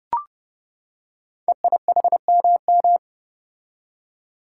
Morse code, heard as a keyed tone, about 30 wpm, 700 Hz. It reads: EIHMM
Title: Code Groups - 5 Letters - 30wpm